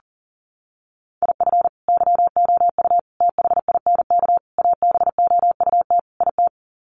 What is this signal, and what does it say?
Morse code, heard as a keyed tone, about 40 wpm, 700 Hz. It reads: IF YOU THINK ABOUT IT
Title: Sets of 5 Words Encore2 40wpm